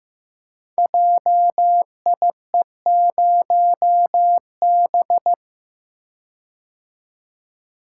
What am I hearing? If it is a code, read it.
Morse code, heard as a keyed tone, about 15 wpm, 700 Hz. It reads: JIE0B
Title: Code Groups - 5 Letter:Numbers - 15wpm